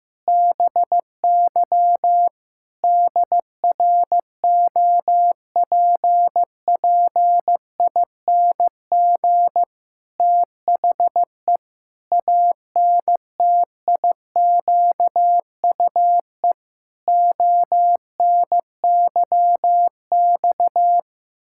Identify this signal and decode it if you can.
Morse code, heard as a keyed tone, about 15 wpm, 700 Hz. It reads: BY DROPPING THE ANTIQUE ONYX
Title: Pangram Challenge v2 - 15wpm